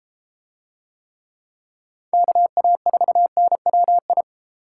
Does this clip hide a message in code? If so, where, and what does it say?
Yes: Morse code from 2.1 to 4.2 s, heard as a keyed tone, about 29 wpm, 700 Hz. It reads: KA4NWI